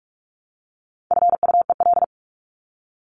Morse code, heard as a keyed tone, about 44 wpm, 700 Hz. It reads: FUEL
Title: Top 1751-2000 Words 45wpm